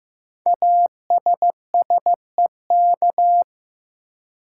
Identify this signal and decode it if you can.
Morse code, heard as a keyed tone, about 15 wpm, 700 Hz. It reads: ASSEK
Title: Code Groups - 5 Letters - 15wpm